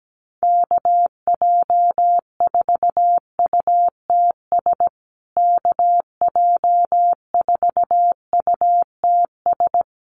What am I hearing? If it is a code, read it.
Morse code, heard as a keyed tone, about 17 wpm, 700 Hz. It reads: KJ4UTS KJ4UTS
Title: USA Call Signs Repeated 3x 17wpm